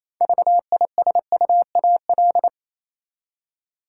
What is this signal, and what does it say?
Morse code, heard as a keyed tone, about 28 wpm, 700 Hz. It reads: VISUAL
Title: Six Letter Words - 28wpm